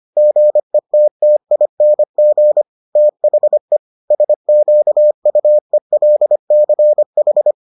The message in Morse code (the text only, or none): GETTING THE SQUELCH